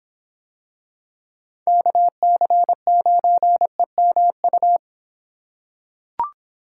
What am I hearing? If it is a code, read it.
Morse code, heard as a keyed tone, about 26 wpm, 700 Hz. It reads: KC9EMU